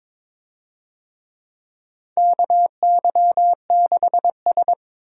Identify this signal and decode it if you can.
Morse code, heard as a keyed tone, about 22 wpm, 700 Hz. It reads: KY6S
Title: USA Call Signs - 4 Characters - Repeated 3x - 22wpm